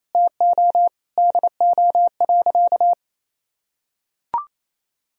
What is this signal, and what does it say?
Morse code, heard as a keyed tone, about 28 wpm, 700 Hz. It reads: TO DO.
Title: Sentences from Top 1000 Words - Review Part 3 - 28wpm